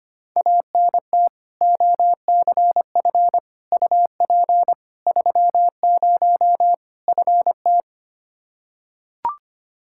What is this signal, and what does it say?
Morse code, heard as a keyed tone, about 25 wpm, 700 Hz. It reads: ANT OCF UP 30 FT
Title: Practice QSO - Exchange 2 - Info - 25wpm